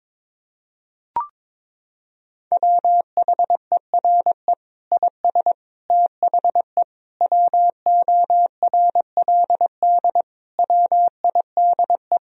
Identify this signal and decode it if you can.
Morse code, heard as a keyed tone, about 22 wpm, 700 Hz. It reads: WHERE IS THE WORLD WIDE
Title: Sentences from Top 1000 Words - Review Part 4 - 22wpm